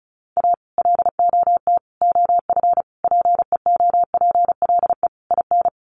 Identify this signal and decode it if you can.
Morse code, heard as a keyed tone, about 34 wpm, 700 Hz. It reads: A LOT OF PEOPLE IN
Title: Sets of 5 Words 35wpm